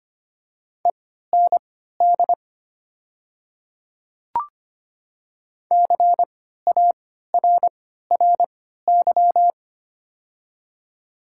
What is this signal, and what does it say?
Morse code, heard as a keyed tone, about 25 wpm, spaced as for 14 wpm, 700 Hz. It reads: END CARRY
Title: 3x Character Spacing - Top 300 Words - 25wpm